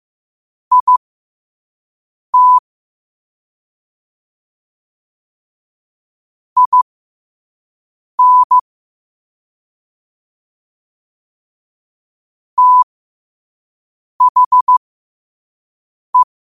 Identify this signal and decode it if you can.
Morse code, heard as a keyed tone, about 15 wpm, spaced as for 5 wpm, 1000 Hz. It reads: IT IN THE
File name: qotdfiles / 5wpm